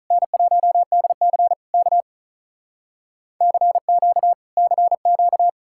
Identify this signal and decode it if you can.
Morse code, heard as a keyed tone, about 46 wpm, 700 Hz. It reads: N1DC K CQ CQ